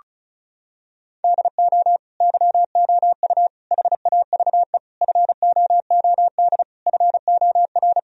Morse code, heard as a keyed tone, about 34 wpm, 700 Hz. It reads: DO YOU HAVE FOOD FOR
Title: Sentences from Top 1000 Words - Review Part 1 - 35wpm